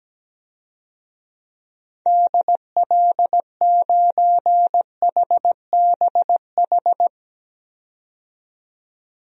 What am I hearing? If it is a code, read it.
Morse code, heard as a keyed tone, about 17 wpm, 700 Hz. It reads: DL9HBH
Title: DXCC-100 Call Signs - 6 Characters - 17wpm